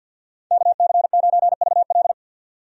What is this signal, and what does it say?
Morse code, heard as a keyed tone, about 47 wpm, 700 Hz. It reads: KC9VL
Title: USA Call Signs - 5 Characters - 50wpm